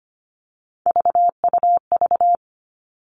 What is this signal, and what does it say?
Morse code, heard as a keyed tone, about 25 wpm, 700 Hz. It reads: VUV
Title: Three Challenging Letter-Number Combinations 25wpm